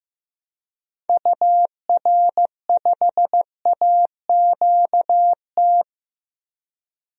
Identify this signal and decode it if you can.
Morse code, heard as a keyed tone, about 15 wpm, 700 Hz. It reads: UR5AQT
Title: DXCC-100 Call Signs - 6 Characters - Repeated 2x - 15wpm